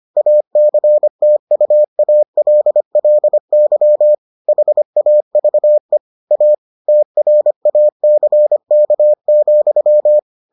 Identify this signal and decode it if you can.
Morse code, heard as a keyed tone, about 25 wpm, 600 Hz. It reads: ACTUALLY HAVE A TRACK,